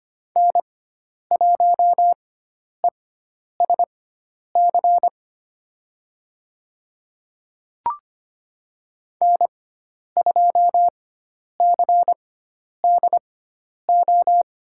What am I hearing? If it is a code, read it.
Morse code, heard as a keyed tone, about 25 wpm, spaced as for 10 wpm, 700 Hz. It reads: N1ESC N2CDO
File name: Lesson 16g - Letter c - Call Signs 25wpm